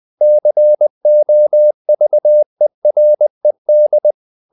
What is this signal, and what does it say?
Morse code, heard as a keyed tone, about 20 wpm, 600 Hz. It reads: COVERED
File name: the old man and the sea / the old man and the sea part 1 20wpm